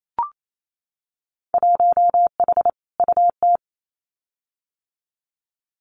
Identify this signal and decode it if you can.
Morse code, heard as a keyed tone, about 28 wpm, 700 Hz. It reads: 1H UT
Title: Winter Field Day Exchanges 28wpm